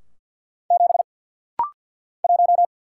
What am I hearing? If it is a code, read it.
Morse code, heard as a keyed tone, about 48 wpm, 700 Hz. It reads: 7 1
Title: Single Letter-Number - Rapid-Fire 50wpm